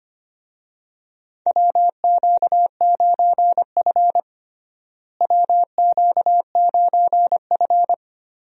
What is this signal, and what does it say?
Morse code, heard as a keyed tone, about 25 wpm, 700 Hz. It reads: WQ9F WQ9F
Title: USA Call Signs - 4 Characters - Repeated 2x - 25wpm